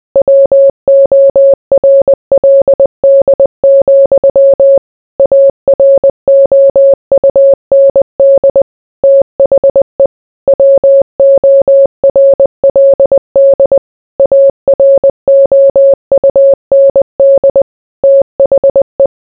Morse code, heard as a keyed tone, about 20 wpm, 550 Hz. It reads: WORLD, AROUND THE WORLD AROUND THE